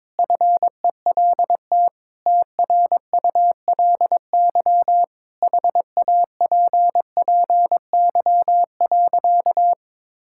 Morse code, heard as a keyed tone, about 22 wpm, 700 Hz. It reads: FELT TRULY HAPPY.
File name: Sentences with 7-Word Phrases - 22wpm